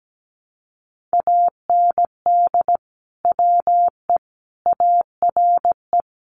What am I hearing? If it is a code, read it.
Morse code, heard as a keyed tone, about 17 wpm, 700 Hz. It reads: AND WE ARE
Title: Sets of 5 Words Encore2 17wpm